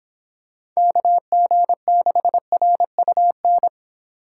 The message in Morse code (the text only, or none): KG6RUN